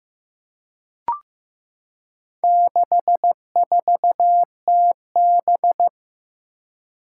Morse code, heard as a keyed tone, about 15 wpm, 700 Hz. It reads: 64TB
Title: Four Challenging Letter-Number Combinations 15wpm